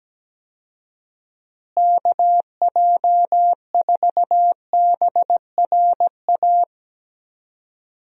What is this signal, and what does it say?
Morse code, heard as a keyed tone, about 17 wpm, 700 Hz. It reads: KJ4BRA
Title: USA Call Signs - 6 Characters - Repeated 3x - 17wpm